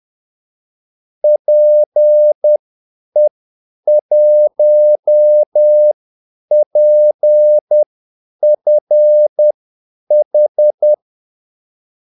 Morse code, heard as a keyed tone, about 10 wpm, spaced as for 8 wpm, 600 Hz. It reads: PE1PFH
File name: Callsigns-10-8wpm